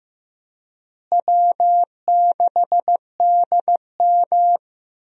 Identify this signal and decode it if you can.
Morse code, heard as a keyed tone, about 15 wpm, 700 Hz. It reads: W6DM
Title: USA Call Signs - 4 Characters - Repeated 3x - 15wpm